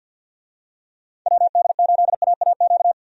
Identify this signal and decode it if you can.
Morse code, heard as a keyed tone, about 49 wpm, 700 Hz. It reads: WD8AAQ
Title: USA Call Signs - 6 Characters - 50wpm